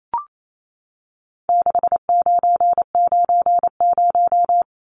Morse code, heard as a keyed tone, about 28 wpm, 700 Hz. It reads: 6990
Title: Numbers - 4-Digit 28wpm